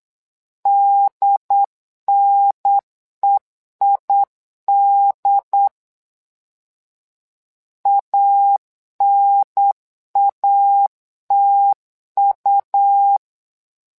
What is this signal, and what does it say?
Morse code, heard as a keyed tone, about 8 wpm, 800 Hz. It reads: DNEID ANATU
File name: LEZIONI / 2 UHDN / 2 UHDN 8wpm2232